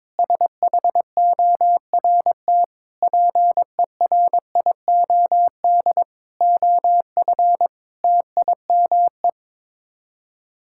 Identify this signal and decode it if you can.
Morse code, heard as a keyed tone, about 22 wpm, 700 Hz. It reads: SHORT PERIOD OF TIME
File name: practice / 22wpm / Sets of 7 Words - 22wpm